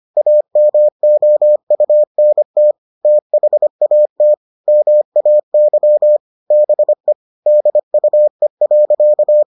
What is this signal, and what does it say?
Morse code, heard as a keyed tone, about 25 wpm, 600 Hz. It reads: AMOUNT THAT MAY BE DUE.